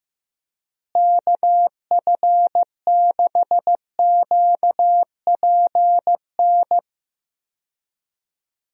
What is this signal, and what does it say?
Morse code, heard as a keyed tone, about 15 wpm, 700 Hz. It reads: KF6QPN
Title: USA Call Signs - 6 Characters - Repeated 2x - 15wpm